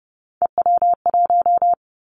Morse code, heard as a keyed tone, about 30 wpm, 700 Hz. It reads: EW1
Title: Callsign Prefixes for Top 50 DXCC - 30wpm